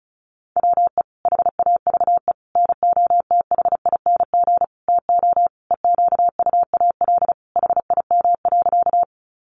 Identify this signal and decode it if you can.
Morse code, heard as a keyed tone, about 34 wpm, 700 Hz. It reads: WE HAVE NOTHING TO EQUAL HIM.
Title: Sentences from Top 1400 Words 35wpm